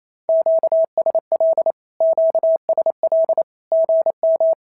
Morse code, heard as a keyed tone, about 28 wpm, 650 Hz. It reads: QSL QSL GM
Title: SOTA QSO Practice 28wpm